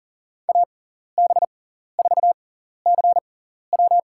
Morse code, heard as a keyed tone, about 37 wpm, spaced as for 14 wpm, 690 Hz. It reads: AB4CW
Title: USA Call Signs - 5 Characters - 6x Character Spacing - 40wpm